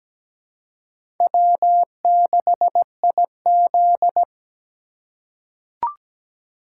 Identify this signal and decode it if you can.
Morse code, heard as a keyed tone, about 17 wpm, 700 Hz. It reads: W6IZ
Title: USA Call Signs - 4 Characters - 3x Character Spacing - 17wpm